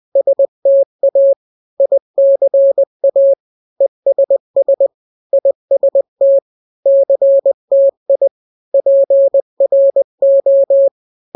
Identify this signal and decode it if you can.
Morse code, heard as a keyed tone, about 20 wpm, 550 Hz. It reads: STA ICA ESS IST CTI PRO
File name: n-grams / 3-grams 20 18 WPM